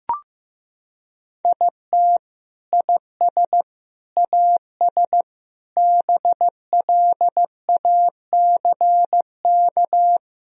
Morse code, heard as a keyed tone, about 15 wpm, 700 Hz. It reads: IT IS AS BLACK
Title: Sentences from Top 1000 Words - Review Part 2 - 15wpm